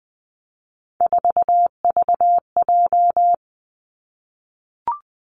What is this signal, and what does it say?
Morse code, heard as a keyed tone, about 20 wpm, 700 Hz. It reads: 4VJ
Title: Three Challenging Letter-Number Combinations 20wpm